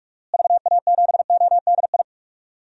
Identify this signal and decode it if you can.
Morse code, heard as a keyed tone, about 48 wpm, 700 Hz. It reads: VA7ODI
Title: DXCC-100 Call Signs - 6 Characters - Repeated 2x - 45wpm